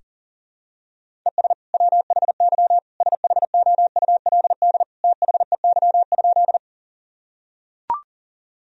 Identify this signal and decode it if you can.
Morse code, heard as a keyed tone, about 37 wpm, 700 Hz. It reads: ES WHY SHOULD THEY?
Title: Sentences from Top 300 Words and Abbreviations 40wpm